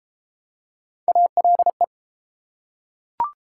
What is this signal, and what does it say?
Morse code, heard as a keyed tone, about 29 wpm, 700 Hz. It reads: ALE